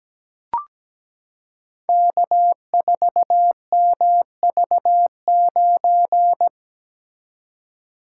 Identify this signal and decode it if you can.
Morse code, heard as a keyed tone, about 17 wpm, 700 Hz. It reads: K4MV9
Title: Code Groups - 5 Letter:Numbers - 17wpm